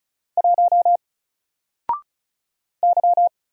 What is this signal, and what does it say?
Morse code, heard as a keyed tone, about 36 wpm, 700 Hz. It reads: 1 Y
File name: Single-Letter-Number-35wpm